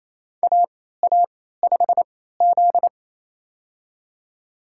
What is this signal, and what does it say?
Morse code, heard as a keyed tone, about 27 wpm, spaced as for 16 wpm, 700 Hz. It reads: AA5Z
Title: USA Call Signs - 4 Characters - 3x Character Spacing - 28wpm